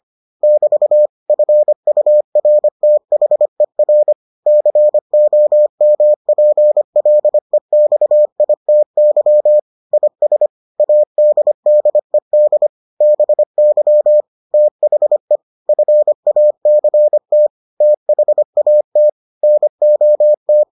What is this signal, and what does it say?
Morse code, heard as a keyed tone, about 25 wpm, 600 Hz. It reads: <BT> FURTHER COMPLEXITY IS ADDED BY THE FACT THAT NOT